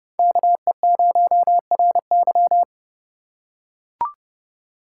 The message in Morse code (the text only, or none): KE0RY